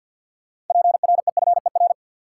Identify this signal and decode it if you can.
Morse code, heard as a keyed tone, about 52 wpm, 700 Hz. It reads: PREFER
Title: Sentences from Top 1100 Words - New Word Drill Alt - 50wpm